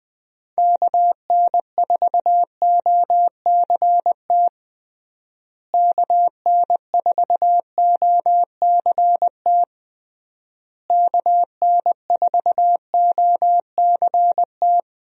Morse code, heard as a keyed tone, about 20 wpm, 700 Hz. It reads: KN4OCT KN4OCT KN4OCT
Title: USA Call Signs - 6 Characters - Repeated 3x - 20wpm